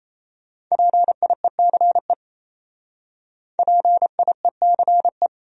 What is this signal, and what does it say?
Morse code, heard as a keyed tone, about 36 wpm, 700 Hz. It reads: PIECE PIECE